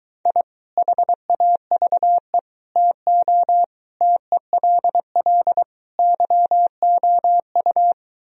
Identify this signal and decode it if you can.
Morse code, heard as a keyed tone, about 23 wpm, 700 Hz. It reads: I HAVE TO TELL YOU